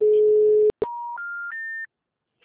Special information tone.
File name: out- -2016- - - .4220